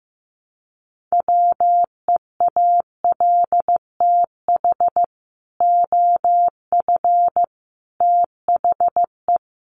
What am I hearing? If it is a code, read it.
Morse code, heard as a keyed tone, about 15 wpm, 700 Hz. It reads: WEALTH OF THE
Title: Sentences from Top 1200 Words 15wpm